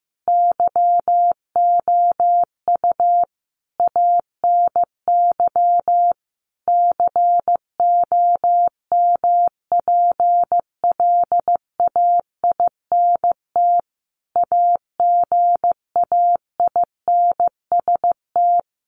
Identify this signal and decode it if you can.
Morse code, heard as a keyed tone, about 15 wpm, 700 Hz. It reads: YOU ANY COMPLAINT AGAINST
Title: Sentences from Top 1400 Words 15wpm